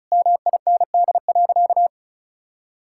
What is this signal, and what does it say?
Morse code, heard as a keyed tone, about 36 wpm, 700 Hz. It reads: MIND.
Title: Sentences from Top 1000 Words - Review Part 5 - 35wpm